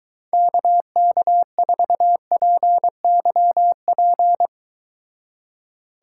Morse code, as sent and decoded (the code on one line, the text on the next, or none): -.- -.- ....- .--. -.-- .--.
KK4PYP